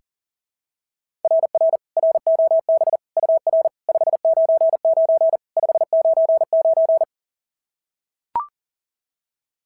Morse code, heard as a keyed tone, about 39 wpm, 690 Hz. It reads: RR ROB UR 599 599
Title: SOTA QSO Practice 40wpm